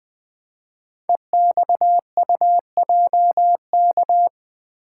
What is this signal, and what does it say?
Morse code, heard as a keyed tone, about 20 wpm, 700 Hz. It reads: EXUJK
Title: Code Groups - 5 Letters - 20wpm